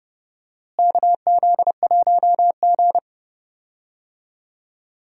Morse code, heard as a keyed tone, about 30 wpm, 700 Hz. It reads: KZ1G
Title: USA Call Signs - 4 Characters - Repeated 3x - 30wpm